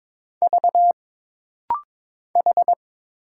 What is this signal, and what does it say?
Morse code, heard as a keyed tone, about 22 wpm, 700 Hz. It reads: V H
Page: This rapid-fire timing is geared to push your ICR (Instant Character Recognition) skills.